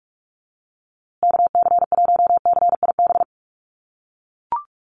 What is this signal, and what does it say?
Morse code, heard as a keyed tone, about 44 wpm, 700 Hz. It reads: KC1CIB
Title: USA Call Signs Repeated 3x 45wpm